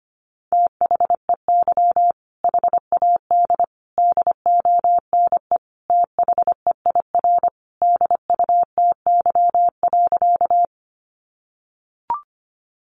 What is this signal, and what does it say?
Morse code, heard as a keyed tone, about 25 wpm, 700 Hz. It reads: THEY HAD DONE THEIR DUTY.
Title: Sentences from Top 1500 Words 25wpm